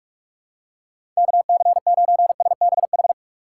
Morse code, heard as a keyed tone, about 46 wpm, 700 Hz. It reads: KC9SBH
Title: USA Call Signs - 2x Character Spacing - 45wpm